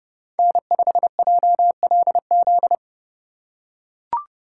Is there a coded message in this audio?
Yes: Morse code from 0.4 to 2.8 s, heard as a keyed tone, about 29 wpm, 700 Hz. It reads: N5JLZ